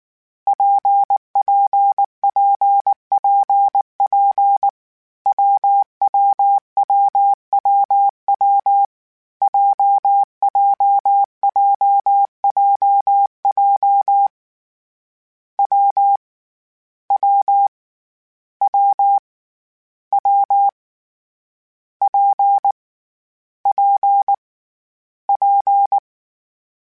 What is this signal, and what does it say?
Morse code, heard as a keyed tone, about 19 wpm, 800 Hz. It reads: PPPPP WWWWW JJJJJ W W W W P P P
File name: Difficult Combos / Difficult Combos-mp3s / WPJ WPJ1 @18